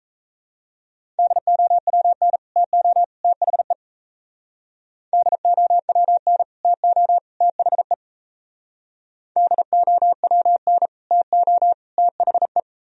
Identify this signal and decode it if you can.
Morse code, heard as a keyed tone, about 34 wpm, 700 Hz. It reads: DOWN TO THE DOWN TO THE DOWN TO THE
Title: Speed-Racing - Sets of 3 Words 28wpm